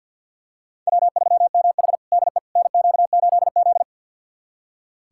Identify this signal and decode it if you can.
Morse code, heard as a keyed tone, about 46 wpm, 690 Hz. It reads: W3MH DE NQ8Z
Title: USA Callsign DE Callsign - 4 Characters - 50wpm